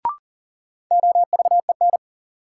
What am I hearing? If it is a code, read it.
Morse code, heard as a keyed tone, about 40 wpm, 700 Hz. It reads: OVEN
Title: Sentences from Top 1200 Words - New Word Drill Alt - 40wpm